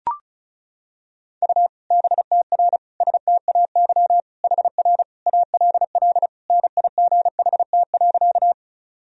Morse code, heard as a keyed tone, about 36 wpm, 700 Hz. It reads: U BTR STAY HR ALL NIGHT.
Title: Sentences from Top 500 Words and Abbreviations 35wpm